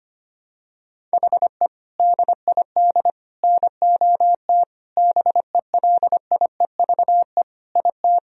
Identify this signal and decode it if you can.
Morse code, heard as a keyed tone, about 25 wpm, 700 Hz. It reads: HE DID NOT BELIEVE IT
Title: Sentences from Top 1000 Words - Review Part 2 - 25wpm